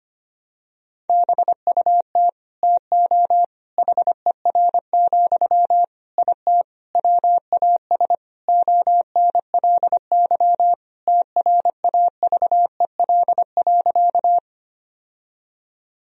Morse code, heard as a keyed tone, about 25 wpm, 700 Hz. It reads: BUT TO HER, IT WAS ONLY TRAVEL.
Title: Sentences from Top 1000 Words - Review Part 2 - 25wpm